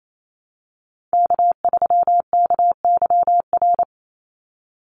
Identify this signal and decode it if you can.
Morse code, heard as a keyed tone, about 28 wpm, 700 Hz. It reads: K3KYR
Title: NA SOTA Chaser Callsigns 28wpm